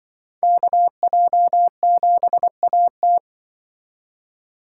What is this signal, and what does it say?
Morse code, heard as a keyed tone, about 24 wpm, 700 Hz. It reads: KJ7AT
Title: Speed Racing - USA 2x2 Call Signs 20wpm